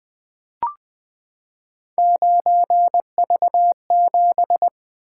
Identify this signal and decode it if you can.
Morse code, heard as a keyed tone, about 20 wpm, 700 Hz. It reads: 9V7
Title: Three Letter-Number Combinations - 20wpm